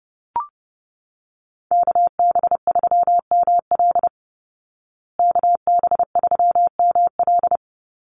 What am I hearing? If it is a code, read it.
Morse code, heard as a keyed tone, about 30 wpm, 700 Hz. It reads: KB3ML KB3ML
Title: Call Signs Repeated 2x - 30wpm